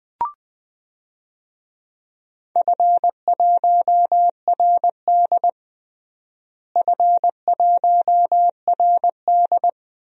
Morse code, heard as a keyed tone, about 20 wpm, 700 Hz. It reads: F1RD F1RD
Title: DXCC-100 Call Signs - 4 Characters - Repeated 3x - 20wpm